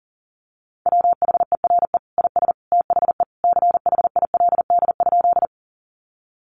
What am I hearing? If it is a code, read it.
Morse code, heard as a keyed tone, about 40 wpm, 700 Hz. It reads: WHERE IS THE CHILD?
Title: Sentences from Top 800 Words 40wpm